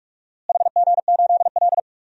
Morse code, heard as a keyed tone, about 46 wpm, 700 Hz. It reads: HG8L
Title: DXCC-100 Call Signs - 4 Characters - Repeated 3x - 45wpm